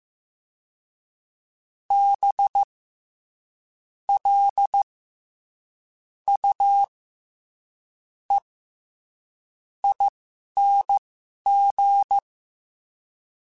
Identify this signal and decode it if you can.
Morse code, heard as a keyed tone, about 15 wpm, spaced as for 11 wpm, 800 Hz. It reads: B L U E ING
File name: Special INGs-mp3s / Special INGs-mp3s / SlowToFaster ing @10